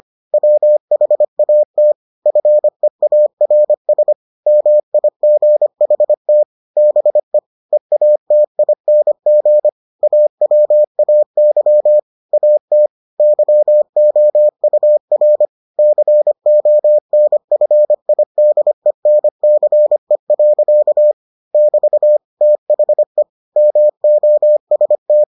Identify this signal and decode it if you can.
Morse code, heard as a keyed tone, about 25 wpm, 600 Hz. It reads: WHAT FEARS MIGHT BE EATING AWAY AT YOUR CONFIDENCE. <BT> THE MOST